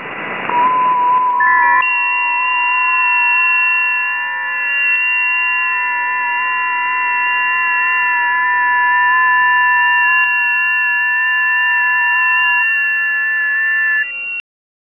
FT8.